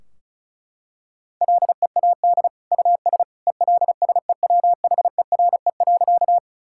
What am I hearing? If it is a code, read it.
Morse code, heard as a keyed tone, about 36 wpm, 700 Hz. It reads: LEAD US ELSEWHERE.
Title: Sentences from Top 1600 Words 35wpm